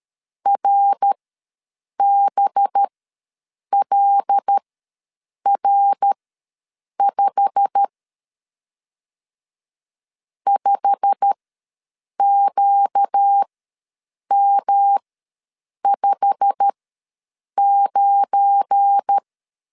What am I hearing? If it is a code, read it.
Morse code, heard as a keyed tone, about 13 wpm, spaced as for 7 wpm, 800 Hz. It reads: RBLR5 5QM59
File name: LEZIONI / 8 YJ45 / 8 YJ45 ws 12wpm 3146